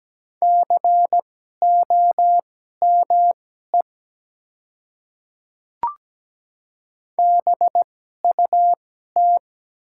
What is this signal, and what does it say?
Morse code, heard as a keyed tone, about 17 wpm, spaced as for 12 wpm, 700 Hz. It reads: COME BUT
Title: 2x Character Spacing - Top 100 Words - 17wpm